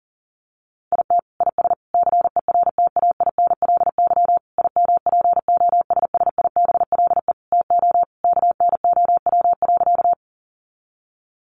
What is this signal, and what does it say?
Morse code, heard as a keyed tone, about 40 wpm, 700 Hz. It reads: IT IS CERTAINLY IMPOSSIBLE TO KNOW.
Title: Sentences from Top 1100 Words 40wpm